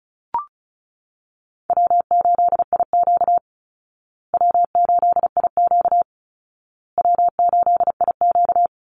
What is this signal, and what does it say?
Morse code, heard as a keyed tone, about 34 wpm, 700 Hz. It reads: W8IQ W8IQ W8IQ
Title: Call Signs Repeated 2x - 35wpm